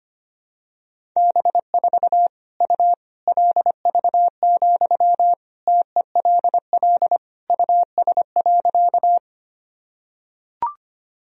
Morse code, heard as a keyed tone, about 25 wpm, 700 Hz. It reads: B4 U LV, TELL US.
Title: Sentences from Top 400 Words and Abbreviations 25wpm